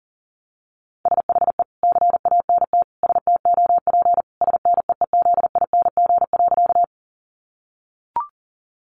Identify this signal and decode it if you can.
Morse code, heard as a keyed tone, about 40 wpm, 700 Hz. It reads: SHE CANT STOP SNEEZING.
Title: Easy Phrases 40wpm